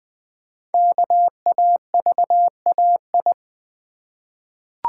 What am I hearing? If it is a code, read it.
Morse code, heard as a keyed tone, about 20 wpm, 700 Hz. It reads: KAVAI
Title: Code Groups - 5 Letters - 20wpm